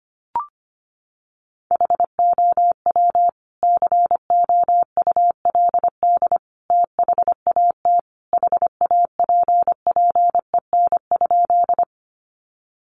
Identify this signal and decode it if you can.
Morse code, heard as a keyed tone, about 25 wpm, 700 Hz. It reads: HOW COULD THAT HAPPEN?
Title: Sentences from Top 400 Words 25wpm